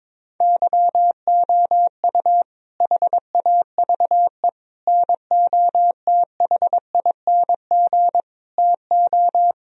Morse code, heard as a keyed tone, about 22 wpm, 700 Hz. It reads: YOU HAVE NOTHING TO
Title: Sentences from Top 1000 Words - Review Part 4 - 22wpm